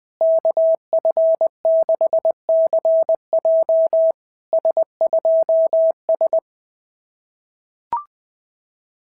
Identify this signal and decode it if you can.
Morse code, heard as a keyed tone, about 18 wpm, 650 Hz. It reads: KF6CJ S2S
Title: SOTA QSO Practice 20wpm